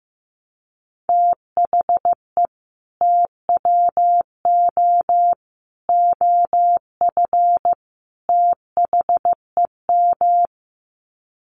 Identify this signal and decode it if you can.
Morse code, heard as a keyed tone, about 15 wpm, 700 Hz. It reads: THE TWO OF THEM
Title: Sets of 4 Words 15wpm